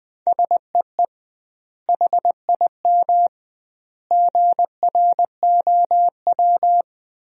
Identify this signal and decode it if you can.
Morse code, heard as a keyed tone, about 20 wpm, 700 Hz. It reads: SEE HIM GROW
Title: 2x Word-Spacing - Sentences from Top 300 Words - 20wpm